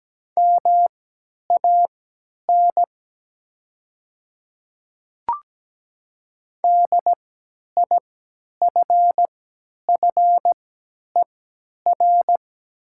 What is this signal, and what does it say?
Morse code, heard as a keyed tone, about 17 wpm, spaced as for 10 wpm, 700 Hz. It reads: MAN DIFFER
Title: 3x Character Spacing - Top 200 Words - 17wpm